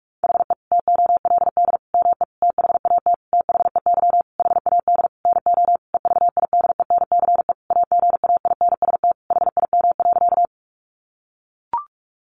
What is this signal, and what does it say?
Morse code, heard as a keyed tone, about 44 wpm, 700 Hz. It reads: HE TOLD ME THAT THEY HAD NO EVIDENCE AGAINST HIM.
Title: Sentences from Top 1300 Words 45wpm